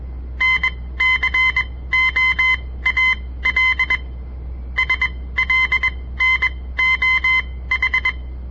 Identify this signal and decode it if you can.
Morse code, heard as a keyed tone, about 21 wpm, spaced as for 16 wpm, 2050 Hz. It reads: NCOAL SLNOH
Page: On the air Morse code practice from the Southern Montana Amatuer Radio Association; all code, no read back